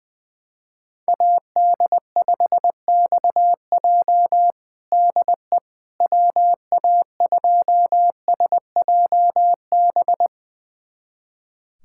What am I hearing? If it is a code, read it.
Morse code, heard as a keyed tone, about 20 wpm, 700 Hz. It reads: AD5XJ DE WA2SJB
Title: USA Callsign DE Callsign 20wpm